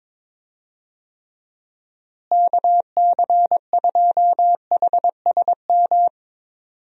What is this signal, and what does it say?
Morse code, heard as a keyed tone, about 22 wpm, 700 Hz. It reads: KC2HSM
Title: USA Call Signs - 6 Characters - Repeated 2x - 22wpm